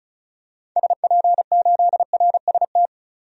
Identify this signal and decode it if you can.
Morse code, heard as a keyed tone, about 34 wpm, 700 Hz. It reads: SP8RST
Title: DXCC-100 Call Signs - 6 Characters - 35wpm